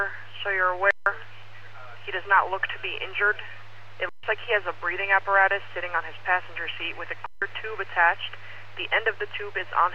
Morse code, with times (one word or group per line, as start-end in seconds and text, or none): none